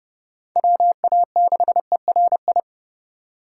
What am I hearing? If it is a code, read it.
Morse code, heard as a keyed tone, about 36 wpm, 700 Hz. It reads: WA6ERI
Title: Speed Racing - USA Call Signs 30wpm